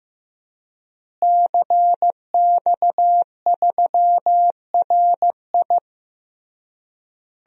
Morse code, heard as a keyed tone, about 15 wpm, 700 Hz. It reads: CX3RI
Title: DXCC-100 Call Signs - 5 Characters - Repeated 3x - 15wpm